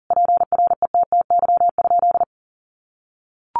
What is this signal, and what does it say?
Morse code, heard as a keyed tone, about 40 wpm, 700 Hz. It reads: PRETTY?
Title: Sentences from Top 1000 Words 40wpm